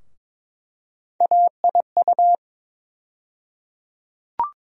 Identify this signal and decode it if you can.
Morse code, heard as a keyed tone, about 21 wpm, 700 Hz. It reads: AIU
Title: USA Call Sign Postfixes - 3 Letters 22wpm